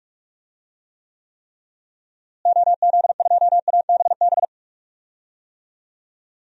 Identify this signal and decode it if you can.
Morse code, heard as a keyed tone, about 46 wpm, 700 Hz. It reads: OZ2ABB
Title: DXCC-100 Call Signs - 6 Characters - Repeated 3x - 45wpm